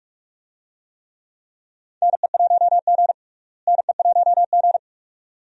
Morse code, heard as a keyed tone, about 42 wpm, 700 Hz. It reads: NE1G NE1G
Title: USA Call Signs - 4 Characters - Repeated 2x - 45wpm